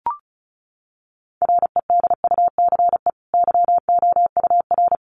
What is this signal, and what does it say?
Morse code, heard as a keyed tone, about 34 wpm, 700 Hz. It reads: REDUCE YOUR
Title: Sentences from Top 1100 Words 35wpm